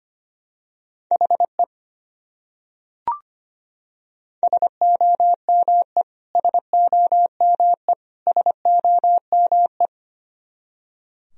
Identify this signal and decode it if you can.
Morse code, heard as a keyed tone, about 25 wpm, 700 Hz. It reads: HE SOME SOME SOME
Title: Top 100 Words Repeated 25wpm